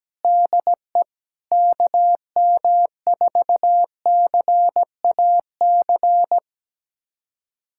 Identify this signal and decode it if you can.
Morse code, heard as a keyed tone, about 17 wpm, 700 Hz. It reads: DE KM4CAC
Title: USA Callsign DE Callsign 17wpm